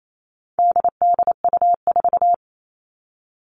Morse code, heard as a keyed tone, about 28 wpm, 700 Hz. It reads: DDU4
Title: All Cut Numbers - 4-Digit 28wpm